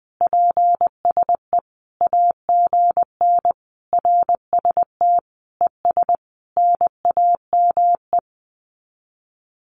Morse code, heard as a keyed tone, about 20 wpm, 700 Hz. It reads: PSE AGN RST ES NAME
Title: Common QSO Phrases 20wpm